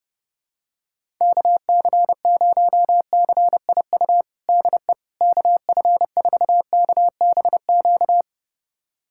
Morse code, heard as a keyed tone, about 30 wpm, 700 Hz. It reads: KC0CIU DE KF4KBQ
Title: USA Callsign DE Callsign - 6 Characters - 30wpm